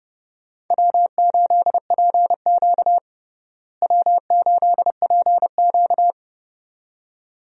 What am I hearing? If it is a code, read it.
Morse code, heard as a keyed tone, about 30 wpm, 700 Hz. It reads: W8PQ W8PQ
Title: USA Call Signs - 4 Characters - Repeated 2x - 30wpm